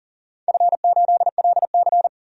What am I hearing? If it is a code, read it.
Morse code, heard as a keyed tone, about 40 wpm, 700 Hz. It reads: F8LC
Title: DXCC-100 Call Signs - 4 Characters - 40wpm